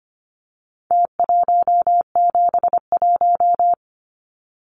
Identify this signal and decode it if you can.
Morse code, heard as a keyed tone, about 25 wpm, 700 Hz. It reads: T171
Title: Numbers - Contest Style - 4-Digit 25wpm